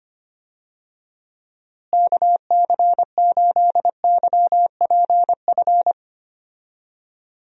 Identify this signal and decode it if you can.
Morse code, heard as a keyed tone, about 25 wpm, 700 Hz. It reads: KC8YPF
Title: USA Call Signs - 6 Characters - Repeated 3x - 25wpm